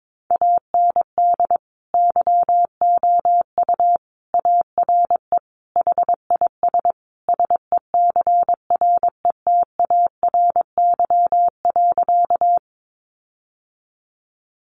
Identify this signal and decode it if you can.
Morse code, heard as a keyed tone, about 22 wpm, 700 Hz. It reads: AND YOU ARE HIS SECRETARY.
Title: Sentences from Top 1300 Words 22wpm